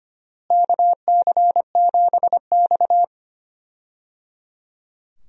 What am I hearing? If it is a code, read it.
Morse code, heard as a keyed tone, about 25 wpm, 700 Hz. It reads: KC7X
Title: USA Call Signs - 4 Characters - Repeated 3x - 25wpm